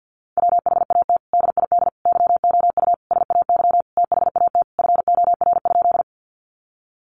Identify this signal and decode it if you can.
Morse code, heard as a keyed tone, about 50 wpm, 700 Hz. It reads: WHAT DID YOU SAY THAT FOR?
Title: Sentences from Top 200 Words 50wpm